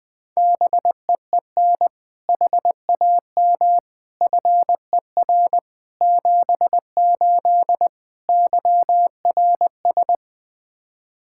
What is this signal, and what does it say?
Morse code, heard as a keyed tone, about 20 wpm, 700 Hz. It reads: BEEN HAM FER 78 YRS
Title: Practice QSO - Exchange 3 - More Info - 20wpm